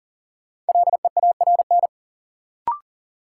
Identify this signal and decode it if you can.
Morse code, heard as a keyed tone, about 40 wpm, 700 Hz. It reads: LEARN
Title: Sentences from Top 300 Words - New Word Drill - 40wpm